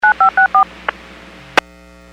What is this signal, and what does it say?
DTMF, heard as telephone touch tones, tones of about 110 ms. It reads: 9564
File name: WPS / 00920 13-20 WPS 451.25 WPS